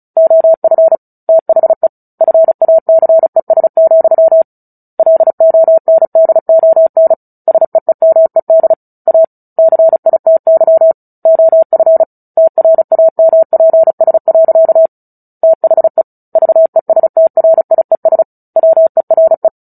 Morse code, heard as a keyed tone, about 34 wpm, 650 Hz. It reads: OF THE FACES, LONDON SEEMED A CITY OF TRAMPS. THE VESTRIES WERE
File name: Book Two / 09-Chapter 9 - 35 WPM